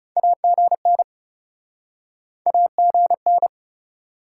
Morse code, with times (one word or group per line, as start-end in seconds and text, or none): 0.2-1.0 AGN
2.5-3.5 AGN